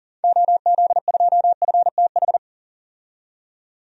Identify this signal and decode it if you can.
Morse code, heard as a keyed tone, about 38 wpm, 700 Hz. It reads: OZ2FTH
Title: DXCC-100 Call Signs - 6 Characters - 40wpm